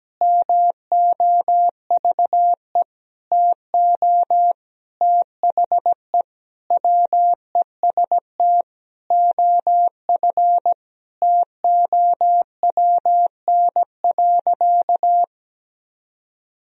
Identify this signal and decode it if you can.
Morse code, heard as a keyed tone, about 17 wpm, 700 Hz. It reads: MOVE TO THE WEST OF TOWN.
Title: Sentences from Top 1000 Words - Review Part 2 - 17wpm